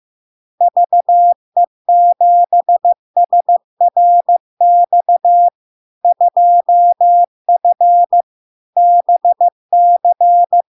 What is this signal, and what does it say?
Morse code, heard as a keyed tone, about 15 wpm, 700 Hz. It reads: VE7SRX 2F BC
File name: LARK-FD-2022-Log-in-CW-15wpm-700-Hz